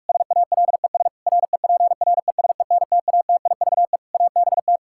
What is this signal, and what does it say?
Morse code, heard as a keyed tone, about 46 wpm, 700 Hz. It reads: SALES REPRESENTATIVE ABT
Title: Practice QSO - Exchange 3 - More Info - 45wpm